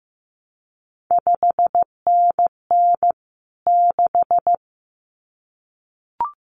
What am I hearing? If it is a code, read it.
Morse code, heard as a keyed tone, about 15 wpm, 700 Hz. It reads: 5NN 6
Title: CQ WW DX Contest Exchange Encore - 15wpm